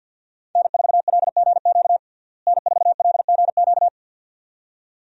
Morse code, heard as a keyed tone, about 51 wpm, 700 Hz. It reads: N4LGQ N4LGQ
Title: USA Call Signs - 5 Characters - Repeated 2x - 50wpm